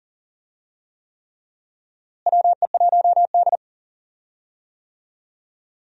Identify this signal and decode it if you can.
Morse code, heard as a keyed tone, about 40 wpm, 700 Hz. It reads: WE1D
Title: USA Call Signs - 4 Characters - Repeated 3x - 40wpm